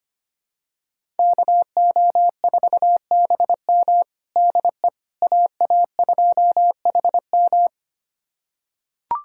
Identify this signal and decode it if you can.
Morse code, heard as a keyed tone, about 25 wpm, 700 Hz. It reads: KO4BM DE AA2HM
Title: USA Callsign DE Callsign - 5 Characters - 25wpm